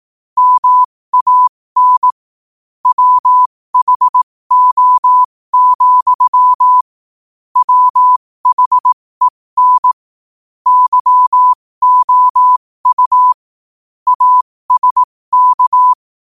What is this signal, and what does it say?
Morse code, heard as a keyed tone, about 18 wpm, spaced as for 15 wpm, 1000 Hz. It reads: MAN WHO, WHEN YOU ASK
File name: qotdfiles / 15wpm